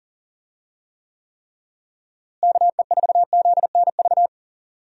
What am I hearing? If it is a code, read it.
Morse code, heard as a keyed tone, about 40 wpm, 700 Hz. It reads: KE4ZNV
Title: USA Call Signs - 6 Characters - Repeated 3x - 40wpm